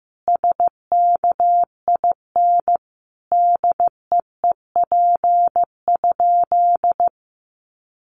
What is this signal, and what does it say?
Morse code, heard as a keyed tone, about 15 wpm, 700 Hz. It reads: SKIN DEEP?
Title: Sentences from Top 1300 Words 15wpm